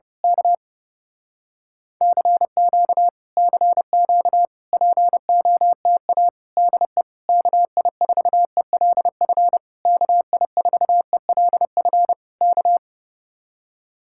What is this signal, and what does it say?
Morse code, heard as a keyed tone, about 28 wpm, 700 Hz. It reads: K CQ CQ POTA DE KI4ELF KI4ELF K
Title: Speed-Racing Common QSO Phrases Encore - 30wpm